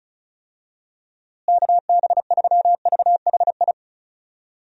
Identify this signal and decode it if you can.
Morse code, heard as a keyed tone, about 34 wpm, 700 Hz. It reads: KB3VHI
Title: USA Call Signs - 6 Characters - Repeated 2x - 35wpm